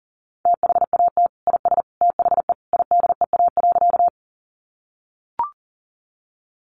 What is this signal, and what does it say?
Morse code, heard as a keyed tone, about 40 wpm, 700 Hz. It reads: THAT IS THE IDEA.
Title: Sentences from Top 300 Words 40wpm